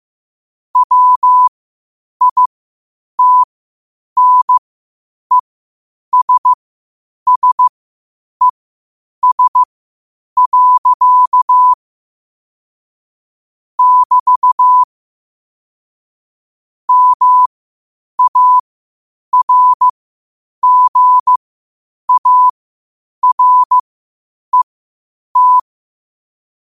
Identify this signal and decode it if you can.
Morse code, heard as a keyed tone, about 15 wpm, spaced as for 8 wpm, 1000 Hz. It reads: WITNESSES. <BT> MARGARET
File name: qotdfiles / 8wpm